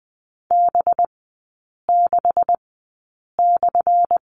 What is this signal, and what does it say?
Morse code, heard as a keyed tone, about 20 wpm, spaced as for 8 wpm, 690 Hz. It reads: B6/
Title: 212 Lesson 34e - Number 8 - Three Characters 20wpm